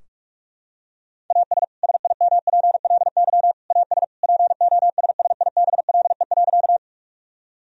Morse code, heard as a keyed tone, about 46 wpm, 700 Hz. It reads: AS SIMPLY AS POSSIBLE.
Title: Sentences from Top 1000 Words - Review Part 4 - 45wpm